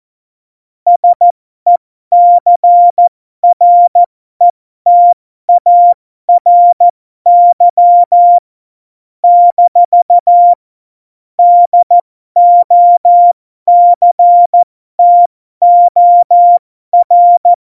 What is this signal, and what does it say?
Morse code, heard as a keyed tone, about 14 wpm, spaced as for 12 wpm, 700 Hz. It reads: SECRETARY - DOCTOR
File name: Morsekurs / Witze